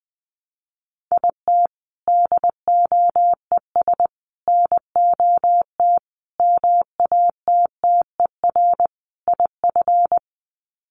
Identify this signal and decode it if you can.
Morse code, heard as a keyed tone, about 20 wpm, 700 Hz. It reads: IT DOES NOT MATTER IF
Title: Sets of 5 Words Encore 20wpm